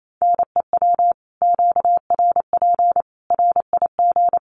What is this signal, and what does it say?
Morse code, heard as a keyed tone, about 28 wpm, 700 Hz. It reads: NEW QRP RIG
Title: Common QSO Phrases 28wpm